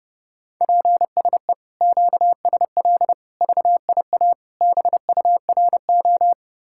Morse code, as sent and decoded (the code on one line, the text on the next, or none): .--. ... / --.- ... .-.. / ...- .. .- / -... ..- .-. ---
PSE QSL VIA BURO